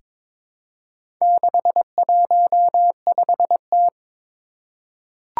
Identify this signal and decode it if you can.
Morse code, heard as a keyed tone, about 22 wpm, 700 Hz. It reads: 615T
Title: Four Challenging Letter-Number Combinations 22wpm